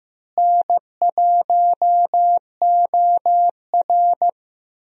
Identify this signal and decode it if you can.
Morse code, heard as a keyed tone, about 16 wpm, 700 Hz. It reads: N1OR
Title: USA Call Signs - 4 Characters - Repeated 2x - 15wpm